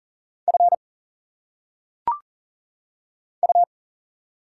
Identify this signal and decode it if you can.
Morse code, heard as a keyed tone, about 41 wpm, 700 Hz. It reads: F U
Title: Single Letters - Phonetic Pronunciation 40wpm